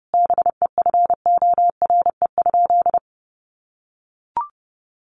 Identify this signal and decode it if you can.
Morse code, heard as a keyed tone, about 30 wpm, 700 Hz. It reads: BEFORE?
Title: Sentences from Top 900 Words 30wpm